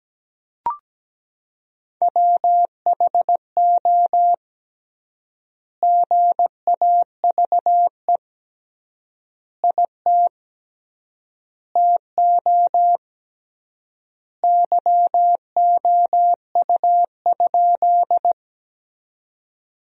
Morse code, heard as a keyed tone, about 17 wpm, 700 Hz. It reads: WHO GAVE IT TO YOU?
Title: 3x Word-Spacing - Sentences from Top 500 Words - 17wpm